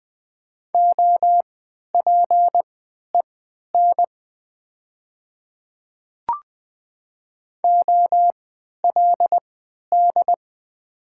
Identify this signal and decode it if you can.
Morse code, heard as a keyed tone, about 20 wpm, spaced as for 11 wpm, 700 Hz. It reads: OPEN OLD
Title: 3x Character Spacing - Top 500 Words - 20wpm